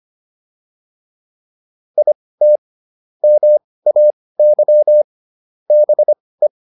Morse code, heard as a keyed tone, about 25 wpm, spaced as for 18 wpm, 600 Hz. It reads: IT MAY BE